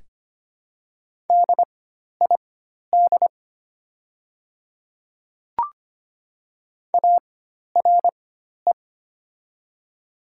Morse code, heard as a keyed tone, about 25 wpm, spaced as for 12 wpm, 700 Hz. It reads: DID ARE
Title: 4x Character Spacing - Top 100 Words - 25wpm